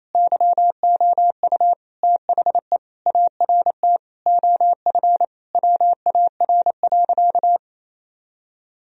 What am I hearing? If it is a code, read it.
Morse code, heard as a keyed tone, about 27 wpm, 700 Hz. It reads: YOU THE ART OF WAR.
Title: Sentences from Top 1000 Words - Review Part 3 - 28wpm